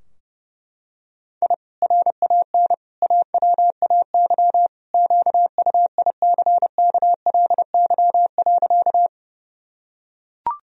Morse code, heard as a keyed tone, about 32 wpm, 700 Hz. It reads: I RAN AWAY QUICKLY.
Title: Sentences from Top 1000 Words - Review Part 3 - 30wpm